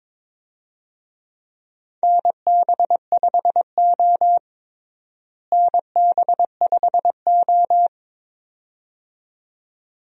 Morse code, heard as a keyed tone, about 22 wpm, 700 Hz. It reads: NB5O NB5O
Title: USA Call Signs - 4 Characters - Repeated 2x - 22wpm